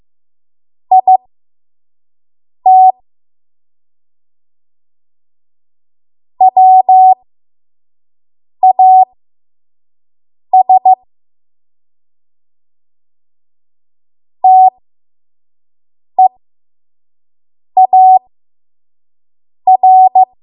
Morse code, heard as a keyed tone, about 15 wpm, spaced as for 5 wpm, 750 Hz. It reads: IT WAS TEAR
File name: Archive / 5 WPM / 05WPM